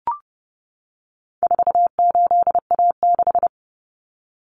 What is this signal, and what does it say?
Morse code, heard as a keyed tone, about 30 wpm, 700 Hz. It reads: 48A6
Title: Common Cut Numbers - 4-Digit 30wpm